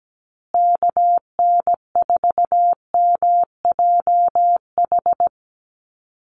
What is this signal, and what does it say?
Morse code, heard as a keyed tone, about 17 wpm, 700 Hz. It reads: KN4MJH
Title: USA Call Signs Repeated 2x 17wpm